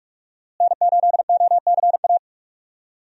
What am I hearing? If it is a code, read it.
Morse code, heard as a keyed tone, about 45 wpm, 700 Hz. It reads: N8OCA
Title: USA Call Signs - 5 Characters - Repeated 3x - 45wpm